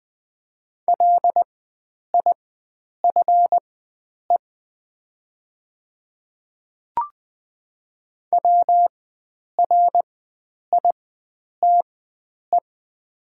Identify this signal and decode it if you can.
Morse code, heard as a keyed tone, about 20 wpm, spaced as for 9 wpm, 700 Hz. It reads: LIFE WRITE
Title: 4x Character Spacing - Top 500 Words - 20wpm